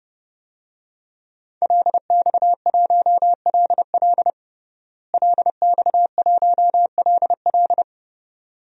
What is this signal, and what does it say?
Morse code, heard as a keyed tone, about 30 wpm, 700 Hz. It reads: LX1LL LX1LL
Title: DXCC-100 Call Signs - 5 Characters - Repeated 2x - 30wpm